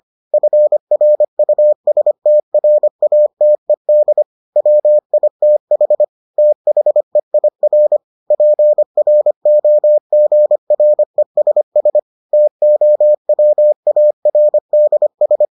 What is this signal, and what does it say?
Morse code, heard as a keyed tone, about 25 wpm, 600 Hz. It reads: FRUSTRATED WITH THEIR PROGRESS TOWARDS